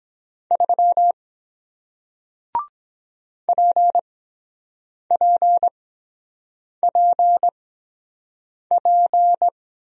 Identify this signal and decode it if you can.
Morse code, heard as a keyed tone, about 23 wpm, spaced as for 7 wpm, 700 Hz. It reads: 3 PPPP